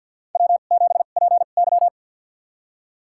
Morse code, heard as a keyed tone, about 50 wpm, spaced as for 36 wpm, 700 Hz. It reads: W7PY
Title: USA Call Signs - 4 Characters - 2x Character Spacing - 50wpm